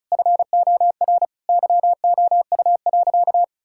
Morse code, heard as a keyed tone, about 34 wpm, 700 Hz. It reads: FOR YOU.
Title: Sentences from Top 1000 Words - Review Part 5 - 35wpm